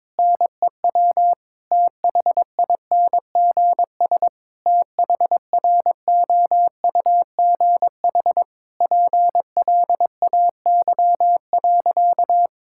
Morse code, heard as a keyed tone, about 21 wpm, 700 Hz. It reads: NEW THINGS THROUGH PLAY.
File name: Difficult Sentences with Hints - 22wpm